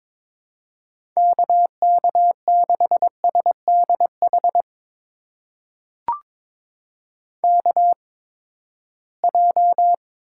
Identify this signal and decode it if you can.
Morse code, heard as a keyed tone, about 22 wpm, spaced as for 15 wpm, 700 Hz. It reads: KK6SDH KJ
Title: USA Call Signs - 6 Characters - 8x Character Spacing - 22wpm